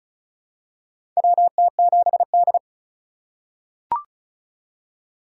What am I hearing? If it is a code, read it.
Morse code, heard as a keyed tone, about 36 wpm, 700 Hz. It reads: WT7D
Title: USA Call Signs - 4x Character Spacing - 35wpm